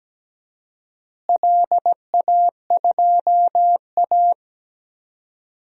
Morse code, heard as a keyed tone, about 17 wpm, 700 Hz. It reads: LA2A
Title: DXCC-100 Call Signs - 4 Characters - Repeated 2x - 17wpm